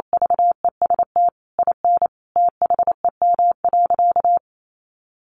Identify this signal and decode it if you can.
Morse code, heard as a keyed tone, about 28 wpm, 700 Hz. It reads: VEST IN THEM.
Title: Sentences from Top 1100 Words 28wpm